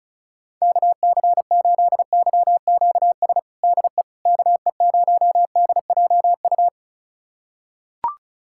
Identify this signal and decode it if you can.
Morse code, heard as a keyed tone, about 36 wpm, 700 Hz. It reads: KC8YQS DE KE0DJU
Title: USA Callsign DE Callsign 35wpm